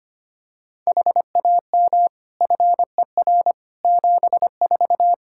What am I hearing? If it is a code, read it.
Morse code, heard as a keyed tone, about 25 wpm, 700 Hz. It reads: HAM FER 74
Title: Practice QSO - Exchange 3 - More Info - 25wpm